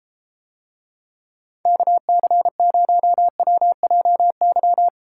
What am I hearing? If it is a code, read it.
Morse code, heard as a keyed tone, about 34 wpm, 700 Hz. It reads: KC0WJY